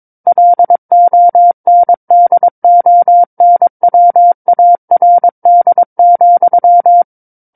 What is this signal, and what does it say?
Morse code, heard as a keyed tone, about 22 wpm, 700 Hz. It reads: LONDONWARD,
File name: Book One / 13-Chapter 13 - 22 WPM